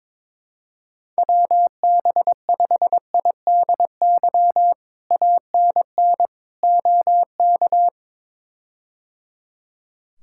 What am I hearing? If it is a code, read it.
Morse code, heard as a keyed tone, about 22 wpm, 700 Hz. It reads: WB5IDY ANN OK
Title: K1USN SST QSO Practice - 22wpm